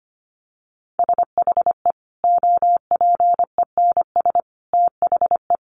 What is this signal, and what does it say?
Morse code, heard as a keyed tone, about 25 wpm, 700 Hz. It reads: SHE OPENS THE
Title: Additional Easy Phrases 25wpm